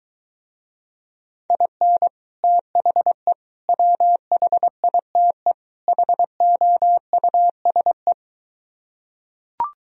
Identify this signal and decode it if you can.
Morse code, heard as a keyed tone, about 23 wpm, 700 Hz. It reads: IN THE WHITE HOUSE